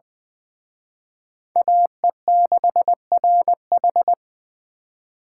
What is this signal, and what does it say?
Morse code, heard as a keyed tone, about 23 wpm, 700 Hz. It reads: AE6RH
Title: Speed Racing - USA 2x2 Call Signs 20wpm